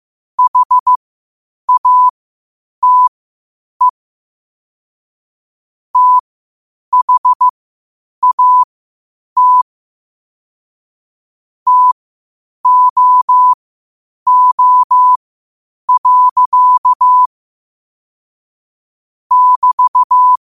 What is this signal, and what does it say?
Morse code, heard as a keyed tone, about 15 wpm, spaced as for 8 wpm, 1000 Hz. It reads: HATE THAT TOO. <BT>
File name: qotdfiles / 8wpm